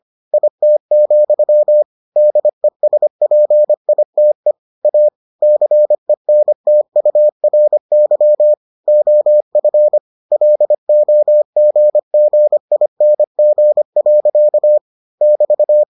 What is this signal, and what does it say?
Morse code, heard as a keyed tone, about 25 wpm, 600 Hz. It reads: IT, DESPITE A CENTURY OF LOGGING. <BT>